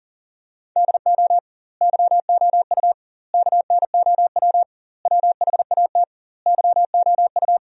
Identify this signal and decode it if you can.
Morse code, heard as a keyed tone, about 40 wpm, 700 Hz. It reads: DO YOU KNOW WHAT YOU
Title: 2x Word-Spacing - Sentences from Top 100 Words - 40wpm